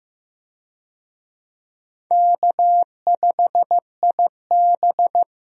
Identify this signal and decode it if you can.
Morse code, heard as a keyed tone, about 16 wpm, 700 Hz. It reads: K5IB
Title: USA Call Signs - 4 Characters - 15wpm